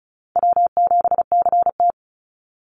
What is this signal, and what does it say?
Morse code, heard as a keyed tone, about 34 wpm, 700 Hz. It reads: W7CT
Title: Easy Call Signs Repeated 3x - 35wpm